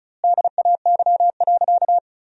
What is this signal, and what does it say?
Morse code, heard as a keyed tone, about 36 wpm, 700 Hz. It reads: DAY.
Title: 2x Word-Spacing - Sentences from Top 500 Words - 35wpm